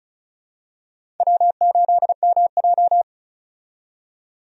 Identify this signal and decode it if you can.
Morse code, heard as a keyed tone, about 36 wpm, 700 Hz. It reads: W8MJ
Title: USA Call Signs - 4 Characters - 35wpm